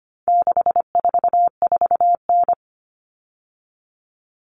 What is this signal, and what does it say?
Morse code, heard as a keyed tone, about 25 wpm, 700 Hz. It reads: 644N
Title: All Cut Numbers - 4-Digit 25wpm